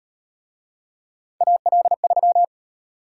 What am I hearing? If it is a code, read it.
Morse code, heard as a keyed tone, about 38 wpm, 700 Hz. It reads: AL3